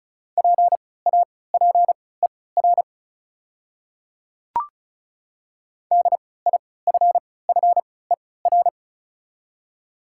Morse code, heard as a keyed tone, about 36 wpm, spaced as for 20 wpm, 700 Hz. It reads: PAPER DIFFER
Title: 3x Character Spacing - Top 300 Words - 35wpm